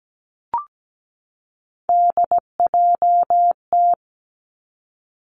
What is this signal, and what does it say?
Morse code, heard as a keyed tone, about 17 wpm, 700 Hz. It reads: DJT
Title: Three Challenging Letter-Number Combinations 17wpm